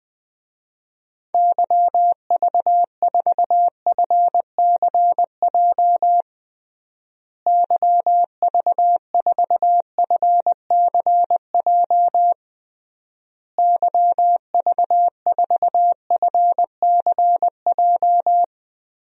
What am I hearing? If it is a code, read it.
Morse code, heard as a keyed tone, about 21 wpm, 700 Hz. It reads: YV4FCJ YV4FCJ YV4FCJ
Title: DXCC-100 Call Signs - 6 Characters - Repeated 3x - 20wpm